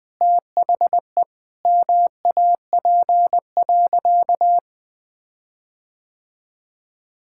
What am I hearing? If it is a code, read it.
Morse code, heard as a keyed tone, about 21 wpm, 700 Hz. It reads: THE MAP.
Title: Sentences with 5-Word Phrases - Spoken Beginning - 20wpm